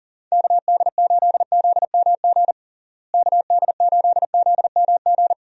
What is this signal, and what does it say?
Morse code, heard as a keyed tone, about 37 wpm, 700 Hz. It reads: KD8ZMG KD8ZMG
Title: USA Call Signs - 6 Characters - Repeated 2x - 40wpm